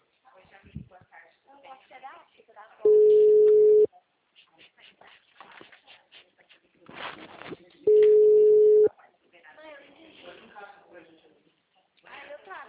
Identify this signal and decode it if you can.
Ringback tone (Europe, 425 Hz), with speech.
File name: out- -2015- - - .544